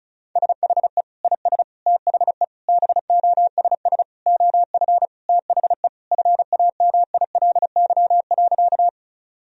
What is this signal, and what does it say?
Morse code, heard as a keyed tone, about 35 wpm, 700 Hz. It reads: SHE IS THE BOSS OF THE FAMILY.
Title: Sentences from Top 1000 Words - Review Part 4 - 35wpm